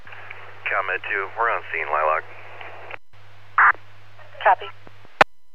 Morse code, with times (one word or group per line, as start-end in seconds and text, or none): none